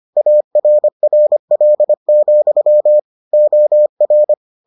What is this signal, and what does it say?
Morse code, heard as a keyed tone, about 25 wpm, 600 Hz. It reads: ARRL, OR